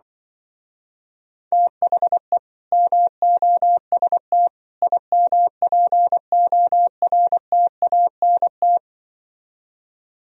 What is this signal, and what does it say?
Morse code, heard as a keyed tone, about 25 wpm, 700 Hz. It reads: THE MOST IMPORTANT
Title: Speed-Racing - Sets of 3 Words 20wpm